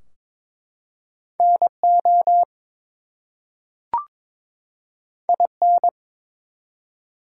Morse code, heard as a keyed tone, about 23 wpm, 700 Hz. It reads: NO IN
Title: Two Letter Words - 22wpm